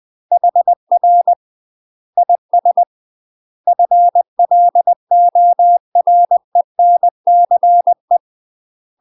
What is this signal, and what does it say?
Morse code, heard as a keyed tone, about 20 wpm, 700 Hz. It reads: HR IS FLORENCE
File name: UK MASTER QSOs TRAINING / 20WPM / UK MASTER QSOs 0008